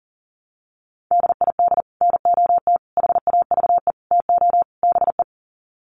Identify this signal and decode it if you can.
Morse code, heard as a keyed tone, about 40 wpm, 700 Hz. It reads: DID NOT HAVE TO BE
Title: Sets of 5 Words Encore2 40wpm